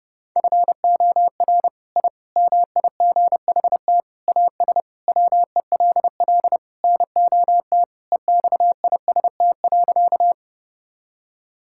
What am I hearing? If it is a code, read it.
Morse code, heard as a keyed tone, about 32 wpm, 700 Hz. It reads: FOR I MIGHT AS WELL NOT EXIST.
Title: Sentences from Top 1000 Words - Review Part 5 - 30wpm